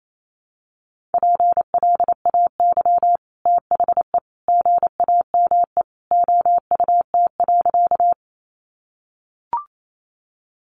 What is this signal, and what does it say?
Morse code, heard as a keyed tone, about 28 wpm, 700 Hz. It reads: PLAY THE GAME OUT.
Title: Sentences from Top 500 Words 28wpm